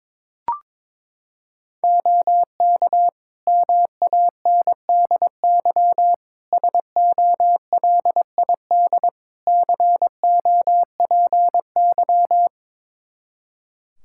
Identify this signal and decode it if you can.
Morse code, heard as a keyed tone, about 22 wpm, 700 Hz. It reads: OK MANDY SOLID COPY
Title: Practice QSO - Exchange 3 - More Info - 22wpm